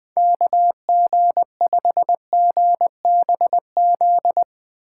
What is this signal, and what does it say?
Morse code, heard as a keyed tone, about 20 wpm, 700 Hz. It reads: KG5GBZ
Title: Speed Racing - USA 2x3 Call Signs 17wpm